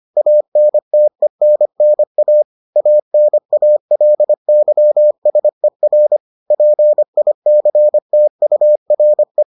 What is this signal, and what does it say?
Morse code, heard as a keyed tone, about 25 wpm, 600 Hz. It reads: ANTENNA ANALYSER PICTURE